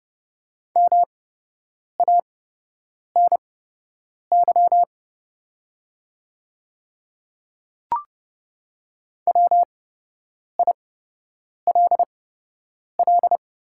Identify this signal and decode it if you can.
Morse code, heard as a keyed tone, about 30 wpm, spaced as for 8 wpm, 700 Hz. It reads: MANY WILL
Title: 8x Character Spacing - Top 200 Words - 30wpm